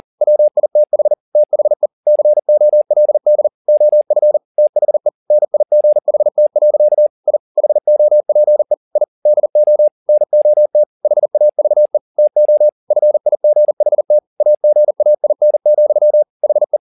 Morse code, heard as a keyed tone, about 40 wpm, 600 Hz. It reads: WITH THE COLD OF THE NIGHT. I HOPE I DO NOT HAVE TO FIGHT AGAIN, HE
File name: the old man and the sea 2 40wpm